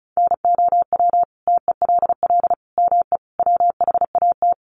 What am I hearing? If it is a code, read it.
Morse code, heard as a keyed tone, about 34 wpm, 700 Hz. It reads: NOW TELL ME WHAT
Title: Sentences from Top 1300 Words 35wpm